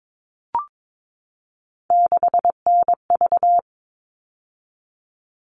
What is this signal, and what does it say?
Morse code, heard as a keyed tone, about 22 wpm, 700 Hz. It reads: 6NV
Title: All Cut Numbers - 3-Digit 22wpm